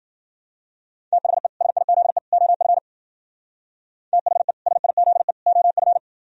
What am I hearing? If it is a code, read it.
Morse code, heard as a keyed tone, about 50 wpm, 700 Hz. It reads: THE SIZE OF THE SIZE OF